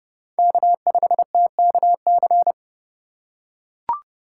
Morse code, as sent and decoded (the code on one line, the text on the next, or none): -.- ..... - -.- -.-.
K5TKC